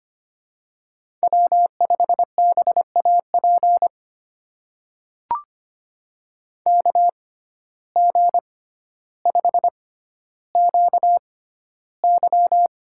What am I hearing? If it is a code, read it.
Morse code, heard as a keyed tone, about 25 wpm, spaced as for 13 wpm, 700 Hz. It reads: W5BAP KG5QY
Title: USA Call Signs - 6x Character Spacing - 25wpm